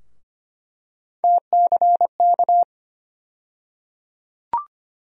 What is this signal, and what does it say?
Morse code, heard as a keyed tone, about 25 wpm, 700 Hz. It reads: TCK
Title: USA Call Sign Postfixes - 3 Letters 25wpm